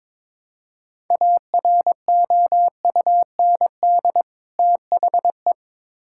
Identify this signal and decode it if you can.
Morse code, heard as a keyed tone, about 22 wpm, 700 Hz. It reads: AROUND THE
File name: Build A Phrase - 2-Word Phrases - 22wpm